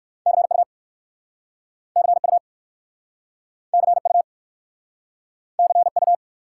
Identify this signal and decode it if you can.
Morse code, heard as a keyed tone, about 59 wpm, 700 Hz. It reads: CU CU CU CU